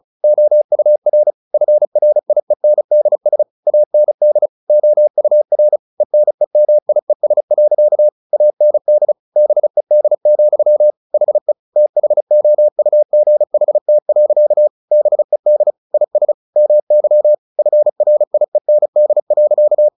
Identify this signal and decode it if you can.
Morse code, heard as a keyed tone, about 35 wpm, 600 Hz. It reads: OUR FRIENDS AND OUR ENEMIES. AND BED, HE THOUGHT. BED IS MY FRIEND.
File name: the old man and the sea 2 35wpm